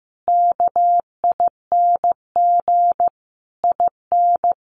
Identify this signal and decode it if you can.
Morse code, heard as a keyed tone, about 15 wpm, 700 Hz. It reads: KING IN
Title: Sentences from Top 1300 Words 15wpm